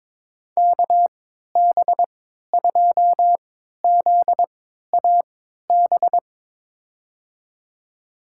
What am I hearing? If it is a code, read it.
Morse code, heard as a keyed tone, about 22 wpm, spaced as for 13 wpm, 700 Hz. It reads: KB2ZAB
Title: USA Call Signs - 6 Characters - 3x Character Spacing - 22wpm